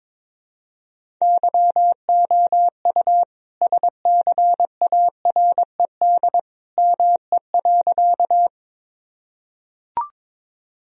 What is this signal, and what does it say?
Morse code, heard as a keyed tone, about 22 wpm, 700 Hz. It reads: YOU SCARED ME.
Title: Sentences from Top 1000 Words - Review Part 4 - 22wpm